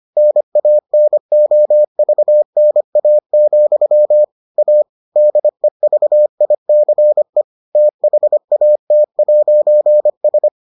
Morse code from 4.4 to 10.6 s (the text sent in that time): A DEVICE THAT'S